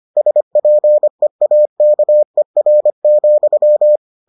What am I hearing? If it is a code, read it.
Morse code, heard as a keyed tone, about 25 wpm, 600 Hz. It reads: SPEAKER,